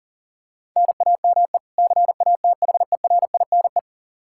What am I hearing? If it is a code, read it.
Morse code, heard as a keyed tone, about 37 wpm, 700 Hz. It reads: NAME CATHERINE
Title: Practice QSO - Exchange 1 - Beginning - 40wpm